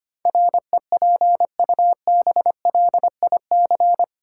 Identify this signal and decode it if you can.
Morse code, heard as a keyed tone, about 25 wpm, 700 Hz. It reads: REPUBLIC
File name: Eight Letter Words - 25wpm